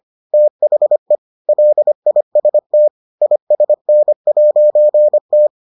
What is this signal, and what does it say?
Morse code, heard as a keyed tone, about 25 wpm, 600 Hz. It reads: THE LIST ISN'T